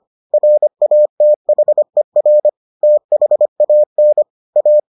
Morse code, heard as a keyed tone, about 25 wpm, 600 Hz. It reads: RATHER THAN A